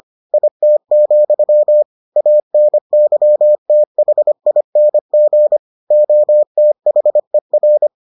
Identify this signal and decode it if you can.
Morse code, heard as a keyed tone, about 25 wpm, 600 Hz. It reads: IT, ANYTHING OTHER